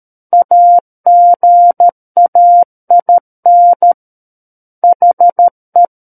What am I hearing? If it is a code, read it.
Morse code, heard as a keyed tone, about 13 wpm, 700 Hz. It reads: AGAIN HE
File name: Book One / 04-Chapter 4 - 13 WPM